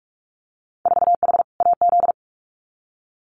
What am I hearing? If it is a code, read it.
Morse code, heard as a keyed tone, about 44 wpm, 690 Hz. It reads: 4H AZ
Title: Winter Field Day Exchanges Encore 45wpm